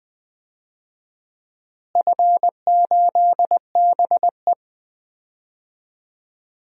Morse code, heard as a keyed tone, about 20 wpm, 700 Hz. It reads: F8BE
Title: DXCC-100 Call Signs - 4 Characters - Repeated 3x - 20wpm